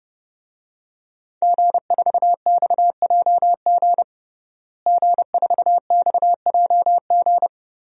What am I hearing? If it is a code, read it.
Morse code, heard as a keyed tone, about 32 wpm, 700 Hz. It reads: G4XJG G4XJG
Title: DXCC-100 Call Signs - 5 Characters - Repeated 3x - 30wpm